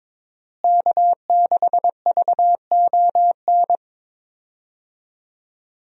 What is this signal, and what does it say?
Morse code, heard as a keyed tone, about 21 wpm, 700 Hz. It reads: K6VON
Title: USA Call Signs - 5 Characters - Repeated 3x - 22wpm